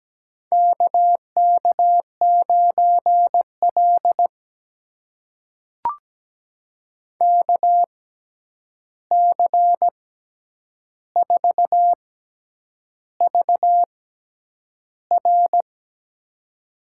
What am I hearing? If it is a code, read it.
Morse code, heard as a keyed tone, about 17 wpm, spaced as for 8 wpm, 700 Hz. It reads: KK9L KC4VR
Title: USA Call Signs - 6x Character Spacing - 17wpm